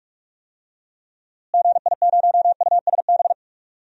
Morse code, heard as a keyed tone, about 46 wpm, 700 Hz. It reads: GI0USB
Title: DXCC-100 Call Signs - 6 Characters - Repeated 3x - 45wpm